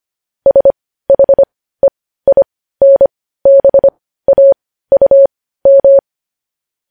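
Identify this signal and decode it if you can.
Morse code, heard as a keyed tone, about 25 wpm, spaced as for 15 wpm, 550 Hz. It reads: SHEINBAUM
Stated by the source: Morse code news headlines sent at 15.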